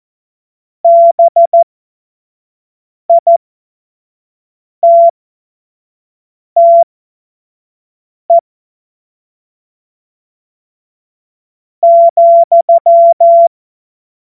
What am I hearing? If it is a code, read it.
Morse code, heard as a keyed tone, about 14 wpm, spaced as for 4 wpm, 680 Hz. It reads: BITTE,
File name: Morsekurs / Witze / Witz 22. Juni 2021, 25 F70